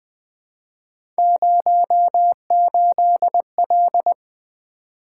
Morse code, heard as a keyed tone, about 20 wpm, 700 Hz. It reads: 08L
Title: Three Letter-Number Combinations - 20wpm